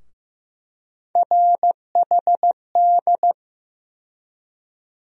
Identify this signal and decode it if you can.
Morse code, heard as a keyed tone, about 15 wpm, 700 Hz. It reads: RHD
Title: USA Call Sign Postfixes - 3 Letters 15wpm